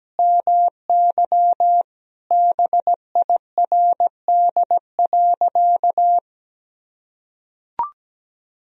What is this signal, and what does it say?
Morse code, heard as a keyed tone, about 17 wpm, 700 Hz. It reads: MY BIRD.
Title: Sentences from Top 1000 Words - Review Part 2 - 17wpm